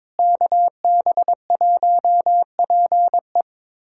Morse code, heard as a keyed tone, about 22 wpm, 700 Hz. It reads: KB1PE
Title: USA Callsign DE Callsign 22wpm